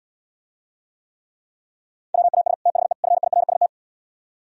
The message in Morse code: WAS THE FIRST

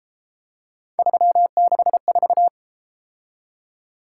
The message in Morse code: 364

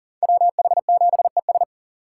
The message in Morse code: WH7ES

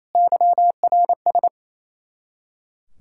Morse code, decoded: YRS

